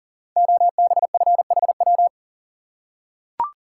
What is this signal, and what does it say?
Morse code, heard as a keyed tone, about 42 wpm, 700 Hz. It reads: OBFHW
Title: Code Groups - 5 Letters - 40wpm